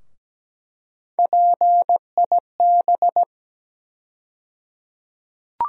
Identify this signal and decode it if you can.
Morse code, heard as a keyed tone, about 17 wpm, 700 Hz. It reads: PIB
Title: USA Call Sign Postfixes - 3 Letters 17wpm